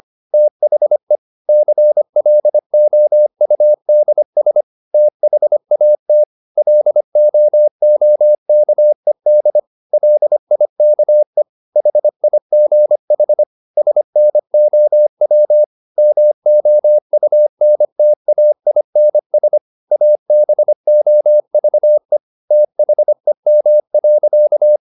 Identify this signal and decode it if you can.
Morse code, heard as a keyed tone, about 25 wpm, 600 Hz. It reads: THE CLOUDS THAT LOOKED LIKE HIGH SNOW MOUNTAINS ABOVE THEM.
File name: the old man and the sea part 1 25wpm